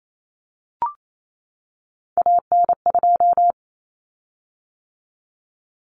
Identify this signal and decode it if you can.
Morse code, heard as a keyed tone, about 28 wpm, 700 Hz. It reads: AN2
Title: Common Cut Numbers - 3-Digit 28wpm